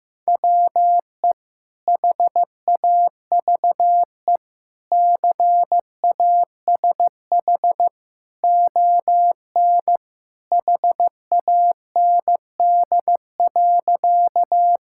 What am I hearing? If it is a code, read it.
Morse code, heard as a keyed tone, about 15 wpm, 700 Hz. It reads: WE HAVE CASH ON HAND.
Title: Sentences from Top 1000 Words - Review Part 5 - 15wpm